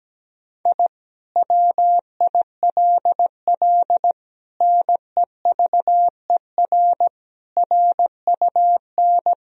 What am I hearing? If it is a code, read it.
Morse code, heard as a keyed tone, about 17 wpm, 700 Hz. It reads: I WILL NEVER RUN
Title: Sentences from Top 1000 Words - Review Part 1 - 17wpm